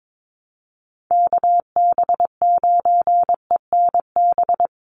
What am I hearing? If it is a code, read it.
Morse code, heard as a keyed tone, about 22 wpm, 700 Hz. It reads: KB9ENB
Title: USA Call Signs Repeated 3x 22wpm